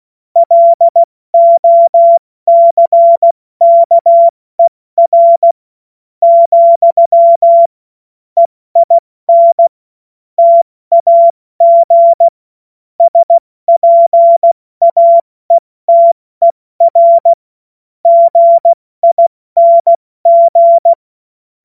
Morse code, heard as a keyed tone, about 16 wpm, spaced as for 14 wpm, 680 Hz. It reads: LOCKER, EIN TAG SPAETER GING
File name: Morsekurs / Witze / Witz 07. September 2021, 70 F80